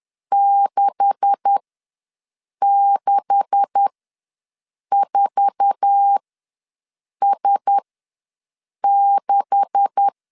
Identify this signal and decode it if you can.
Morse code, heard as a keyed tone, about 10 wpm, spaced as for 6 wpm, 800 Hz. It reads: 664S6
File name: LEZIONI / 9 286 ws 10wpm 4032